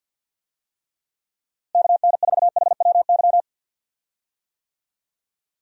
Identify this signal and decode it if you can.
Morse code, heard as a keyed tone, about 53 wpm, 700 Hz. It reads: KN4HWY
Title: USA Call Signs - 6 Characters - Repeated 3x - 50wpm